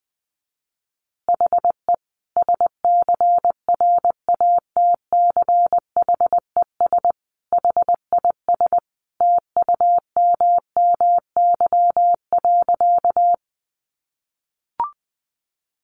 Morse code, heard as a keyed tone, about 20 wpm, 700 Hz. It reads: HE SCRATCHES HIS TUMMY.
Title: Easy Phrases 20wpm